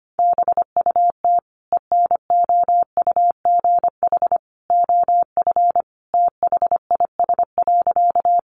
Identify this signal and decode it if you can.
Morse code, heard as a keyed tone, about 25 wpm, 700 Hz. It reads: BUT ENOUGH OF THIS.
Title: Sentences from Top 400 Words 25wpm